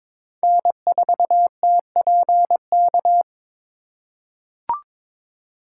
Morse code, heard as a keyed tone, about 22 wpm, 700 Hz. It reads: N4TPK
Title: USA Call Signs - 2x Character Spacing - 22wpm